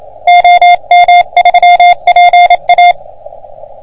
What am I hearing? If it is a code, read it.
Morse code, heard as a keyed tone, about 27 wpm, 710 Hz. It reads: OM3PA
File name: OM3PA PILUP CUC 388